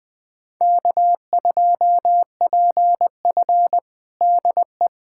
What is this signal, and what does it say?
Morse code, heard as a keyed tone, about 20 wpm, 700 Hz. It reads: K2PF DE
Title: USA Callsign DE Callsign - 4 Characters - 20wpm